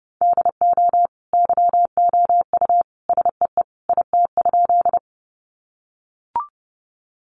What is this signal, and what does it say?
Morse code, heard as a keyed tone, about 30 wpm, 700 Hz. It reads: DO YOU SEE IT?
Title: Sentences from Top 100 Words 30wpm